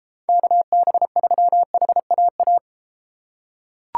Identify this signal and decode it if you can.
Morse code, heard as a keyed tone, about 31 wpm, 700 Hz. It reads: KB3HAA